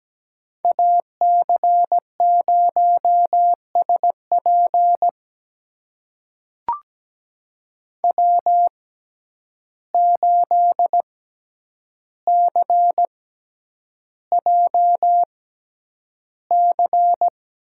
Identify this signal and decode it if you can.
Morse code, heard as a keyed tone, about 17 wpm, spaced as for 9 wpm, 700 Hz. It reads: AC0SP W8CJC
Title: USA Call Signs - 5 Characters - 6x Character Spacing - 17wpm